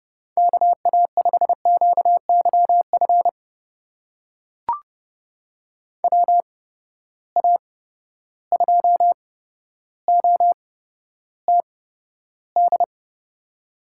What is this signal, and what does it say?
Morse code, heard as a keyed tone, about 30 wpm, spaced as for 13 wpm, 700 Hz. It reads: KA5QYF WA2OTD
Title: USA Call Signs - 6 Characters - 8x Character Spacing - 30wpm